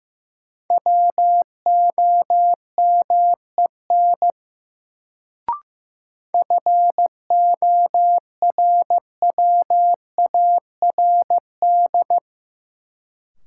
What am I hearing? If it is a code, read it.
Morse code, heard as a keyed tone, about 15 wpm, 700 Hz. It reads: WOMEN FORWARD
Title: Sentences from Top 1300 Words - New Word Drill - 15wpm